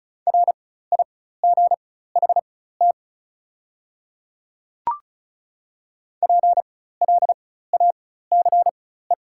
Morse code, heard as a keyed tone, about 36 wpm, spaced as for 16 wpm, 700 Hz. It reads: RIGHT PLACE
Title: 4x Character Spacing - Top 200 Words - 35wpm